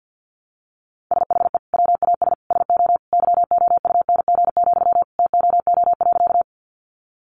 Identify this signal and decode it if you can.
Morse code, heard as a keyed tone, about 50 wpm, 700 Hz. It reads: SHE WAS SO YOUNG, TOO.
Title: Sentences from Top 400 Words 50wpm